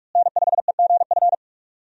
Morse code, heard as a keyed tone, about 46 wpm, 700 Hz. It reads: N5EGF
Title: USA Call Signs - 5 Characters - Repeated 2x - 45wpm